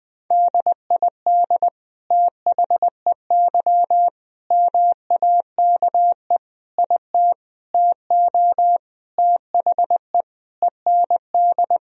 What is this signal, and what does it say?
Morse code, heard as a keyed tone, about 20 wpm, 700 Hz. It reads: DID THEY MAKE IT TO THE END
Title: Sentences from Top 1000 Words - Review Part 3 - 20wpm